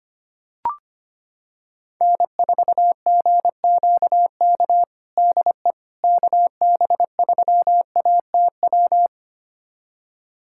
Morse code, heard as a keyed tone, about 25 wpm, 700 Hz. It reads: N4GQK DE KB3ATW
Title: USA Callsign DE Callsign 25wpm